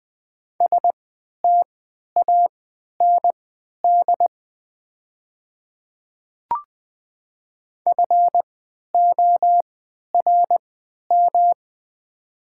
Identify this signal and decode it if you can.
Morse code, heard as a keyed tone, about 21 wpm, spaced as for 11 wpm, 700 Hz. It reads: STAND FORM
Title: 3x Character Spacing - Top 500 Words - 20wpm